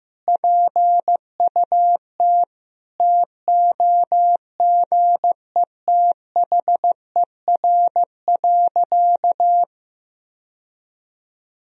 Morse code, heard as a keyed tone, about 16 wpm, 700 Hz. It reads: PUT TOGETHER.
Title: Sentences from Top 1000 Words - Review Part 3 - 15wpm